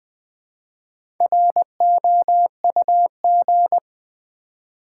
Morse code, heard as a keyed tone, about 20 wpm, 700 Hz. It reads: ROUG
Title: Build A Word - 1 Letter At A Time - 3 to 5-Letter Words - 20wpm